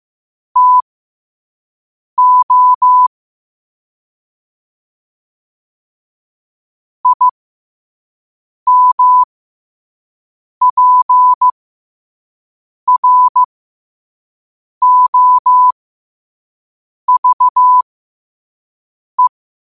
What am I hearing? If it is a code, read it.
Morse code, heard as a keyed tone, about 15 wpm, spaced as for 5 wpm, 1000 Hz. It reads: TO IMPROVE
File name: qotdfiles / 5wpm